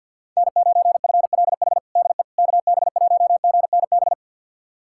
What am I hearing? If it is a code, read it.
Morse code, heard as a keyed tone, about 52 wpm, 700 Hz. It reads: N9FLH DE KB1GNB
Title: USA Callsign DE Callsign 50wpm